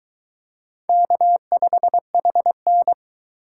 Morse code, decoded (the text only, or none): K5HN